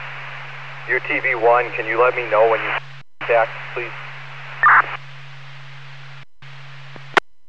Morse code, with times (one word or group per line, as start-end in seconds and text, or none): none